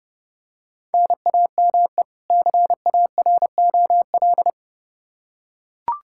Morse code, heard as a keyed tone, about 32 wpm, 700 Hz. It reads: NAME CAROL
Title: Practice QSO - Exchange 1 - Beginning - 30wpm